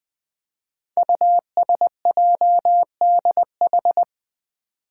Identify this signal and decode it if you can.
Morse code, heard as a keyed tone, about 20 wpm, 700 Hz. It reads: USJDH
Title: Code Groups - 5 Letters - 20wpm